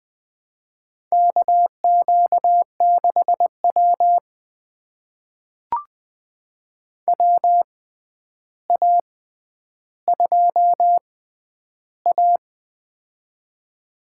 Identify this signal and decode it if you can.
Morse code, heard as a keyed tone, about 20 wpm, spaced as for 10 wpm, 700 Hz. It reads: KQ6W WA2A
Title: USA Call Signs - 4 Characters - 6x Character Spacing - 20wpm